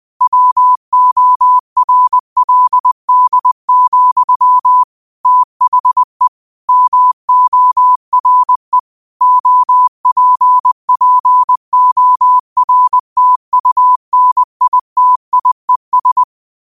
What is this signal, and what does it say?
Morse code, heard as a keyed tone, about 20 wpm, 1000 Hz. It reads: WORLD, THE MORE OPPORTUNITIES
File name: qotdfiles / 20wpm